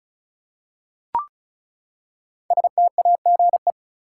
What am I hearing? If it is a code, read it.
Morse code, heard as a keyed tone, about 36 wpm, 700 Hz. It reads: STAGE
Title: Five Letter Words - 35wpm